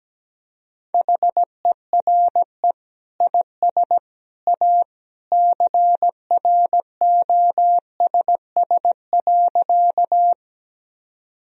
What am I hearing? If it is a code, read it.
Morse code, heard as a keyed tone, about 17 wpm, 700 Hz. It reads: HERE IS A CROSS.
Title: Sentences from Top 1000 Words - Review Part 1 - 17wpm